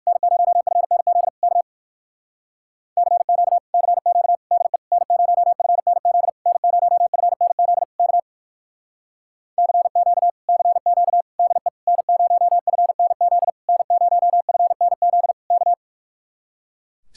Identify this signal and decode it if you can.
Morse code, heard as a keyed tone, about 59 wpm, 700 Hz. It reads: N0FNZ K CQ CQ DE N0FNZ N0FNZ K CQ CQ DE N0FNZ N0FNZ K